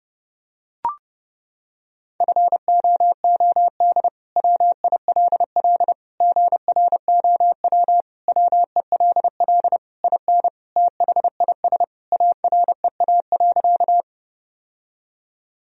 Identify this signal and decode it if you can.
Morse code, heard as a keyed tone, about 30 wpm, 700 Hz. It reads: FOOD WILL GROW WELL IN THIS AREA.
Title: Sentences from Top 1000 Words - Review Part 2 - 30wpm